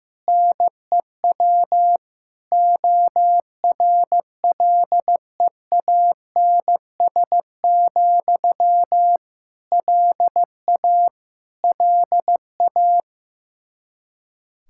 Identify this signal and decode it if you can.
Morse code, heard as a keyed tone, about 15 wpm, 700 Hz. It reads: NEW ORLEANS, LA LA
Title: Practice QSO - Exchange 1 - Beginning - 15wpm